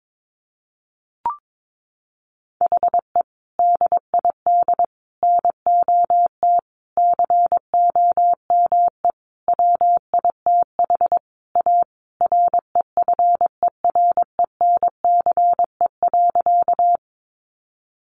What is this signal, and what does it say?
Morse code, heard as a keyed tone, about 22 wpm, 700 Hz. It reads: HE DID NOT COME WITH A REFERENCE.
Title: Sentences from Top 1200 Words 22wpm